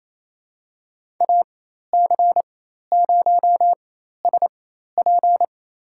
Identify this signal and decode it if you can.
Morse code, heard as a keyed tone, about 28 wpm, spaced as for 13 wpm, 700 Hz. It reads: AC0SP
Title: USA Call Signs - 4x Character Spacing - 28wpm